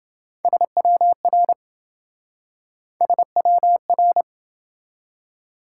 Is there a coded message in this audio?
Yes: Morse code from 0.4 to 4.2 s, heard as a keyed tone, about 28 wpm, 700 Hz. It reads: SWR SWR